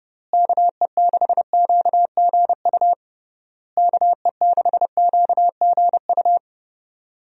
Morse code, heard as a keyed tone, about 32 wpm, 700 Hz. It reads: KE6QGU KE6QGU
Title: USA Call Signs - 6 Characters - Repeated 3x - 30wpm